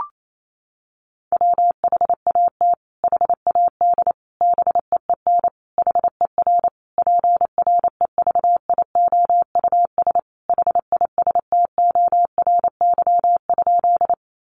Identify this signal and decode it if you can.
Morse code, heard as a keyed tone, about 28 wpm, 700 Hz. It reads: WHAT HAD BEEN HER PREVIOUS HISTORY?
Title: Sentences from Top 1500 Words 28wpm